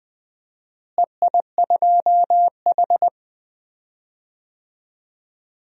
Morse code, heard as a keyed tone, about 20 wpm, 700 Hz. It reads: EI2H
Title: DXCC-100 Call Signs - 4 Characters - Repeated 3x - 20wpm